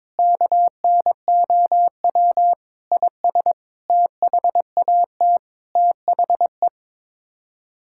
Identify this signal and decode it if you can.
Morse code, heard as a keyed tone, about 22 wpm, 700 Hz. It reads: KNOW IS THAT THE
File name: practice / 22wpm / Sets of 7 Words - 22wpm